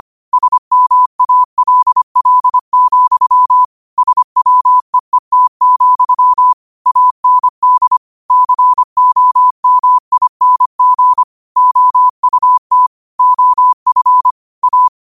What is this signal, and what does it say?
Morse code, heard as a keyed tone, about 25 wpm, 1000 Hz. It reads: SMALL, SWEET, AND COMING OUT OF A
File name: qotdfiles / 25wpm